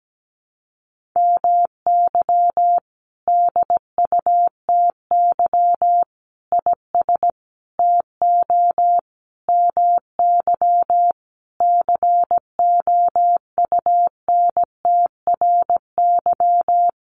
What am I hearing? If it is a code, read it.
Morse code, heard as a keyed tone, about 17 wpm, 700 Hz. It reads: MY DUTY IS TO MY COUNTRY
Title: Sentences from Top 1500 Words 17wpm